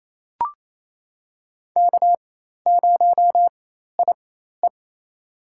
Morse code, heard as a keyed tone, about 27 wpm, spaced as for 13 wpm, 700 Hz. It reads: K0IE
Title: USA Call Signs - 4 Characters - 4x Character Spacing - 28wpm